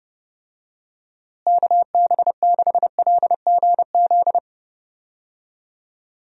Morse code, heard as a keyed tone, about 30 wpm, 700 Hz. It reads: KB6LGZ
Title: USA Call Signs - 6 Characters - Repeated 2x - 30wpm